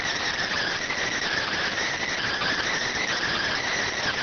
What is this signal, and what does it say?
Emergency Alert System (SAME) tones, heard as harsh buzzing bursts. It is heard from 0.6 to 3.8 s.